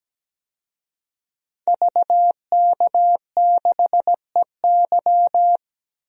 Morse code, heard as a keyed tone, about 17 wpm, 700 Hz. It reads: VK6EY
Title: DXCC-100 Call Signs - 5 Characters - Repeated 2x - 17wpm